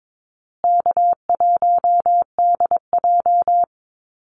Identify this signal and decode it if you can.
Morse code, heard as a keyed tone, about 22 wpm, 700 Hz. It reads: K1DJ
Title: Easy Call Signs Repeated 2x - 22wpm